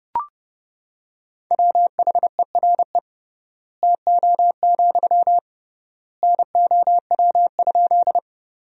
Morse code, heard as a keyed tone, about 30 wpm, 700 Hz. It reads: WHERE TO, NOW?
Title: 3x Word-Spacing - Sentences from Top 200 Words - 30wpm